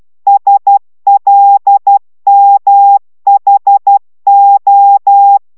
Morse code, heard as a keyed tone, about 12 wpm, 800 Hz. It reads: SLMHO